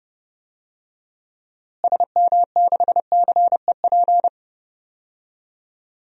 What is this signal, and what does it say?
Morse code, heard as a keyed tone, about 32 wpm, 700 Hz. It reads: SM6CEP
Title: DXCC-100 Call Signs - 6 Characters - Repeated 3x - 30wpm